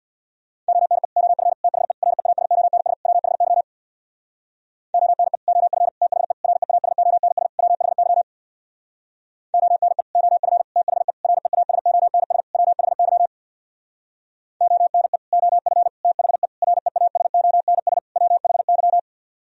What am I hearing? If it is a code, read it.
Morse code, heard as a keyed tone, about 62 wpm, 700 Hz. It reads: ONE OF THE REASONS WHY ONE OF THE REASONS WHY ONE OF THE REASONS WHY ONE OF THE REASONS WHY
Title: Speed-Racing - Sets of 5 Words 50wpm